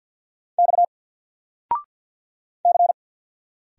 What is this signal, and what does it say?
Morse code, heard as a keyed tone, about 50 wpm, 700 Hz. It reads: X C
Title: Single Letter-Number 50wpm